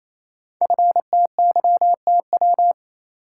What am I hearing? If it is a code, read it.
Morse code, heard as a keyed tone, about 29 wpm, 700 Hz. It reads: FTYTW
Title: Code Groups - 5 Letters - 28wpm